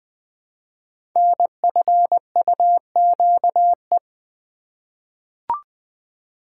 Morse code, heard as a keyed tone, about 20 wpm, 700 Hz. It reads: NFUQE
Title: Code Groups - 5 Letter:Numbers - 20wpm